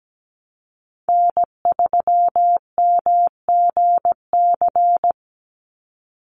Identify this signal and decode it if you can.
Morse code, heard as a keyed tone, about 17 wpm, 700 Hz. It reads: N3MGC
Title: USA Call Signs 17wpm